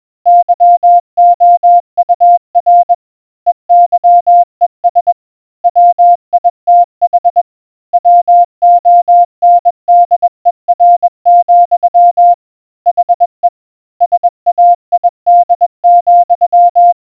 Morse code, heard as a keyed tone, about 21 wpm, 700 Hz. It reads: YOUR EYES WITH WONDER, HE SAID,